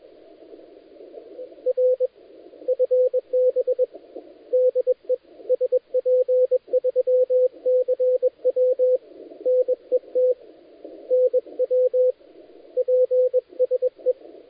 Morse code, heard as a keyed tone, about 23 wpm, 510 Hz, with radio noise. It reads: R FB DE SP3CW NET NW PSE
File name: dlazaawansowanych / nagraniazpasma / titawaNETcontrol